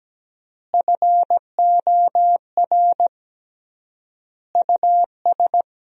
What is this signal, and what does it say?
Morse code, heard as a keyed tone, about 17 wpm, 700 Hz. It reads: FOR US
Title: 3x Word-Spacing - Sentences from Top 500 Words - 17wpm